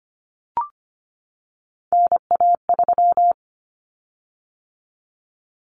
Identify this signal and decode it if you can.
Morse code, heard as a keyed tone, about 25 wpm, 700 Hz. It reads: NA3
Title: Common Cut Numbers - 3-Digit 25wpm